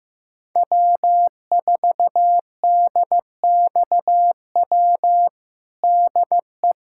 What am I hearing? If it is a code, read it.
Morse code, heard as a keyed tone, about 15 wpm, 700 Hz. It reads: W4DXW DE
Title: USA Callsign DE Callsign - 5 Characters - 15wpm